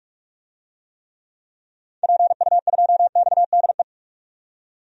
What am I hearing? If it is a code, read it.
Morse code, heard as a keyed tone, about 46 wpm, 700 Hz. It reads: PU2XDE
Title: DXCC-100 Call Signs - 6 Characters - 45wpm